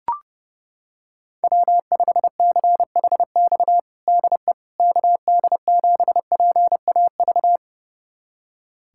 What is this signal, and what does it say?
Morse code, heard as a keyed tone, about 32 wpm, 700 Hz. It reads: W5CHX DE KD7PAV
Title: USA Callsign DE Callsign 30wpm